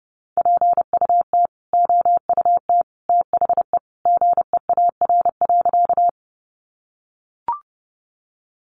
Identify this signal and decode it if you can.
Morse code, heard as a keyed tone, about 30 wpm, 700 Hz. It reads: PUT OUT THE GEAR.
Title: Sentences from Top 1400 Words 30wpm